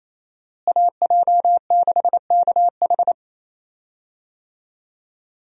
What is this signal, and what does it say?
Morse code, heard as a keyed tone, about 28 wpm, 700 Hz. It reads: AJ6KH
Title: USA Call Signs - 5 Characters - Repeated 3x - 28wpm